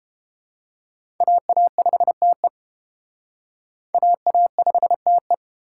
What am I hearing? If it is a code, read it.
Morse code, heard as a keyed tone, about 30 wpm, 700 Hz. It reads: AA5TE AA5TE